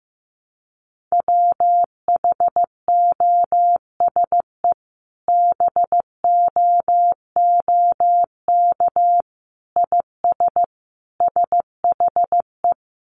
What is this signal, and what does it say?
Morse code, heard as a keyed tone, about 15 wpm, 700 Hz. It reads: WHOSE BOOK IS SHE
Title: Sentences from Top 1200 Words 15wpm